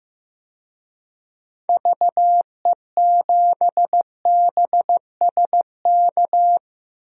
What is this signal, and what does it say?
Morse code, heard as a keyed tone, about 15 wpm, 700 Hz. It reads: VE7BSK
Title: DXCC-100 Call Signs - 6 Characters - 15wpm